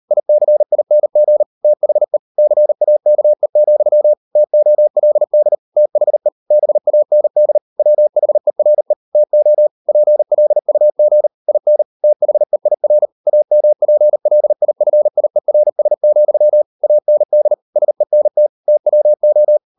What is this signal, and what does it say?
Morse code, heard as a keyed tone, about 38 wpm, 600 Hz. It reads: ICING THE CAKE, TOLD THE BAND WHERE TO PLUG IN THEIR AMPLIFIERS, AND SENT TWO